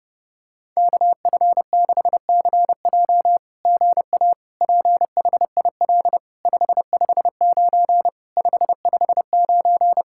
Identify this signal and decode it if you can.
Morse code, heard as a keyed tone, about 29 wpm, 700 Hz. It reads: KF6CJ GA PHIL 559 559
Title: SOTA QSO Practice 30wpm